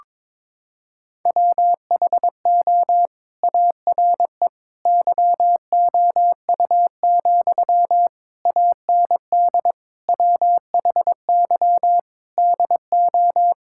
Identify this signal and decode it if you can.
Morse code, heard as a keyed tone, about 22 wpm, 700 Hz. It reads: WHO ARE YOU, AND WHY DO
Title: Sentences from Top 1600 Words 22wpm